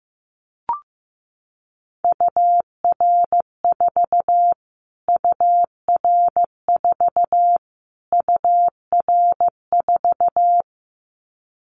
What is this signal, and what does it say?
Morse code, heard as a keyed tone, about 14 wpm, 700 Hz. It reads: UR4 UR4 UR4
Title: Callsign Prefixes for Top 25 DXCC - Repeated 3x - 15wpm